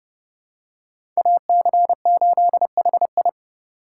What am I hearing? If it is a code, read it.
Morse code, heard as a keyed tone, about 30 wpm, 700 Hz. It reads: AC8HI
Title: USA Call Signs - 5 Characters - 3x Character Spacing - 30wpm